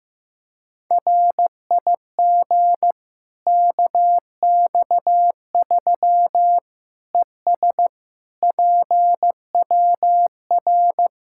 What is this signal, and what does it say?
Morse code, heard as a keyed tone, about 15 wpm, 700 Hz. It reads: RIG KX3 ES PWR
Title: K1USN SST QSO Practice 15wpm